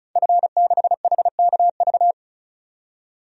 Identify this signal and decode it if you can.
Morse code, heard as a keyed tone, about 34 wpm, 700 Hz. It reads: F6HKV
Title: DXCC-100 Call Signs - 5 Characters - 35wpm